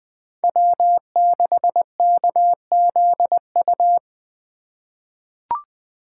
Morse code, heard as a keyed tone, about 20 wpm, 700 Hz. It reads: W6KZU
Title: USA Call Signs - 4x Character Spacing - 20wpm